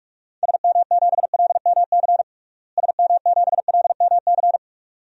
Morse code, heard as a keyed tone, about 46 wpm, 700 Hz. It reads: SM7LMC SM7LMC
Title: DXCC-100 Call Signs - 6 Characters - Repeated 2x - 45wpm